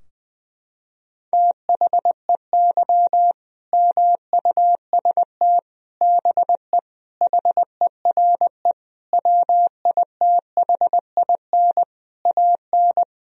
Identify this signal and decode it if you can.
Morse code, heard as a keyed tone, about 20 wpm, 700 Hz. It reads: THEY MUST BE HERE WITHIN AN
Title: Sentences from Top 1000 Words - Review Part 4 - 20wpm